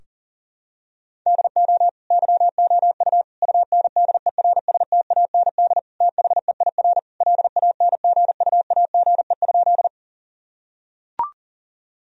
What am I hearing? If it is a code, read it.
Morse code, heard as a keyed tone, about 40 wpm, 700 Hz. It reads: DO YOU UNDERSTAND THEIR LANGUAGE?
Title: Sentences from Top 1000 Words - Review Part 3 - 40wpm